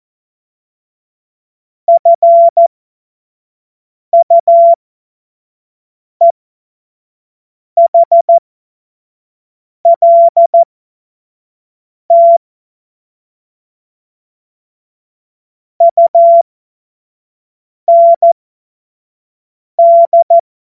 Morse code, heard as a keyed tone, about 14 wpm, spaced as for 5 wpm, 680 Hz. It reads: FUEHLT UND
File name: Morsekurs / Witze / Witz 02 April 2024 25 F70